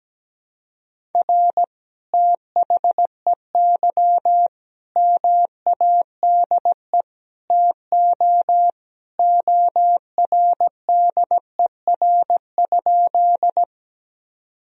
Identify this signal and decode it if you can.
Morse code, heard as a keyed tone, about 17 wpm, 700 Hz. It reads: R THEY MADE TO ORDER?
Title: Sentences from Top 400 Words and Abbreviations 17wpm